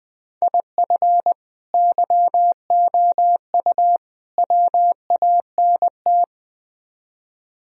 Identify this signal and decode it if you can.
Morse code, heard as a keyed tone, about 20 wpm, 700 Hz. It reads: IF YOU WANT